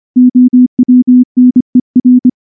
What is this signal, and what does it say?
Morse code, heard as a keyed tone, about 24 wpm, 250 Hz. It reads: OWNER
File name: morse / morse-training-25WPM